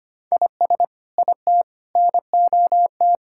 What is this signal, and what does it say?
Morse code, heard as a keyed tone, about 25 wpm, 700 Hz. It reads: IS IT NOT
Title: Sentences from Top 1600 Words 25wpm